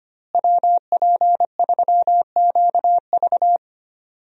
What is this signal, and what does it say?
Morse code, heard as a keyed tone, about 25 wpm, 700 Hz. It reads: WP3QV
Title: USA Call Signs - 5 Characters - Repeated 3x - 25wpm